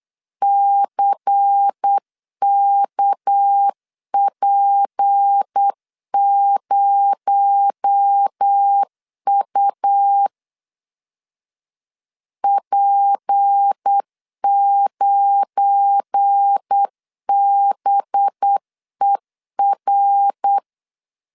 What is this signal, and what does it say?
Morse code, heard as a keyed tone, about 8 wpm, 800 Hz. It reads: CKP0U P9BER
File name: LEZIONI / 7 GF9Z / 7 GF9Z 8wpm 3213